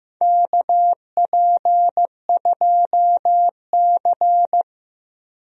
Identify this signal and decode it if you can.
Morse code, heard as a keyed tone, about 15 wpm, 700 Hz. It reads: KP2C
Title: DXCC-100 Call Signs - 4 Characters - 15wpm